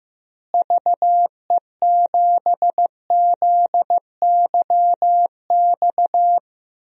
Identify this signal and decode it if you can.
Morse code, heard as a keyed tone, about 15 wpm, 700 Hz. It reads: VE7ZYX
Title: DXCC-100 Call Signs - 6 Characters - Repeated 2x - 15wpm